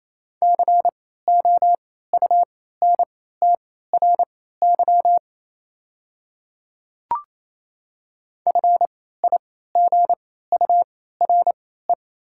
Morse code, heard as a keyed tone, about 28 wpm, spaced as for 16 wpm, 690 Hz. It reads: COUNTRY FIGURE
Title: 3x Character Spacing - Top 500 Words - 28wpm